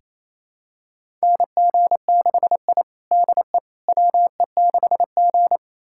Morse code, heard as a keyed tone, about 28 wpm, 700 Hz. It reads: NG6I DE WE6G
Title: USA Callsign DE Callsign - 4 Characters - 28wpm